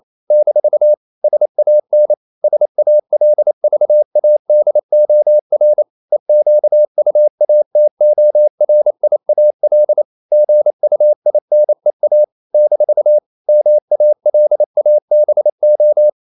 Morse code, heard as a keyed tone, about 28 wpm, 600 Hz. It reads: - SAN SALVADOR EQUATORIAL GUINEA - MALABO
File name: CW / Practice / NationalCapitals28